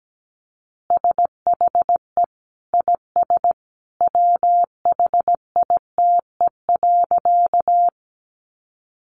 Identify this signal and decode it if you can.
Morse code, heard as a keyed tone, about 17 wpm, 700 Hz. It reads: SHE IS WHITE.
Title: Easy Phrases 17wpm